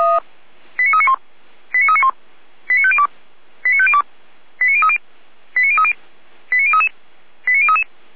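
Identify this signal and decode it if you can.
Five-tone selective call (ZVEI), heard as quick runs of five tones, tones of about 70 ms. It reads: 89391 89391 89502 89502 89030 89030 89033 89033
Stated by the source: Five-tone ZVEI selective call 89050, received via SDR.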